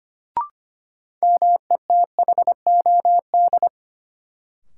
Morse code, heard as a keyed tone, about 25 wpm, 700 Hz. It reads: METHOD
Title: Sentences from Top 600 Words - New Word Drill - 25wpm